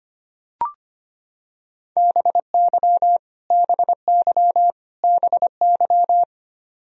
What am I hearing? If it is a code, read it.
Morse code, heard as a keyed tone, about 25 wpm, 700 Hz. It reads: BY BY BY
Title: Top 100 Words Repeated 25wpm